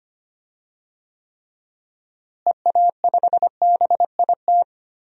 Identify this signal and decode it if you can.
Morse code, heard as a keyed tone, about 25 wpm, 700 Hz. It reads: EA5BIT
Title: DXCC-100 Call Signs - 6 Characters - Repeated 2x - 25wpm